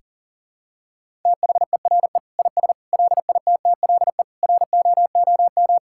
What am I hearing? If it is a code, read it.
Morse code, heard as a keyed tone, about 37 wpm, 700 Hz. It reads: THERE IS LITTLE ROOM
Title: Sentences from Top 1000 Words - Review Part 6 - 40wpm